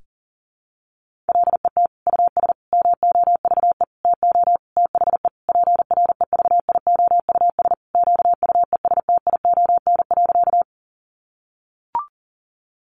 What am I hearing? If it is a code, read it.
Morse code, heard as a keyed tone, about 40 wpm, 700 Hz. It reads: LET US MOVE TO THE PREVIOUS QUESTION.
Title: Sentences from Top 1500 Words 40wpm